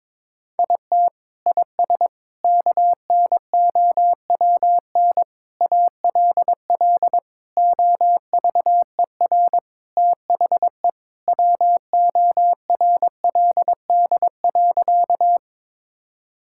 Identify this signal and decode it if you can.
Morse code, heard as a keyed tone, about 22 wpm, 700 Hz. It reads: IT IS KNOWN ALL OVER THE WORLD.
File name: practice / 22wpm / Sentences with 4-Word Phrases - 22wpm